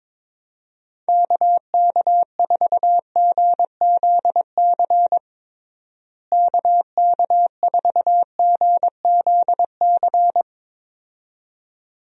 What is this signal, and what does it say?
Morse code, heard as a keyed tone, about 22 wpm, 700 Hz. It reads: KK4GZC KK4GZC
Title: USA Call Signs - 6 Characters - Repeated 2x - 22wpm